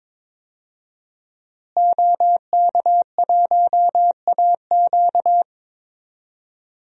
Morse code, heard as a keyed tone, about 22 wpm, 700 Hz. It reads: OK1AQ
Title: DXCC-100 Call Signs - 5 Characters - Repeated 2x - 22wpm